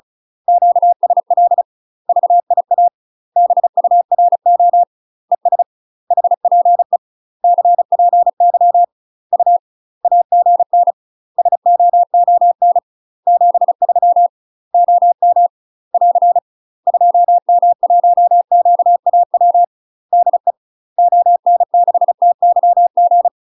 Morse code, heard as a keyed tone, about 35 wpm, 700 Hz. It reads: QSL VIA BURO ES HPE CPY U AGN SOON 73 OM <AR> 2M1QAW DE ON6TYG
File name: UK MASTER QSOs TRAINING / 35WPM / UK MASTER QSOs 0006